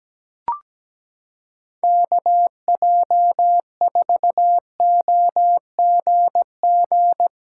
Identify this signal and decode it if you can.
Morse code, heard as a keyed tone, about 17 wpm, 700 Hz. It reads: KJ4OGG
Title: USA Callsign DE Callsign - 6 Characters - 17wpm